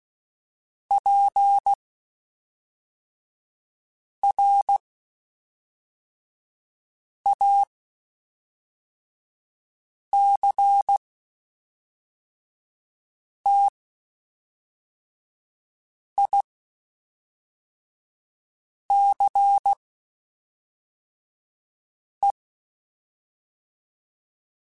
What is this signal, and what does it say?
Morse code, heard as a keyed tone, about 16 wpm, spaced as for 3 wpm, 800 Hz. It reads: PRACTICE
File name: Quotes Spaced / Quotes Spaced-mp3s / Deceive sp @5